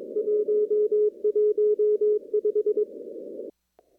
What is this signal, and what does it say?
Morse code, heard as a keyed tone, about 24 wpm, 410 Hz. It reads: 115
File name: CW / kt5j wpx cw-op-k5pi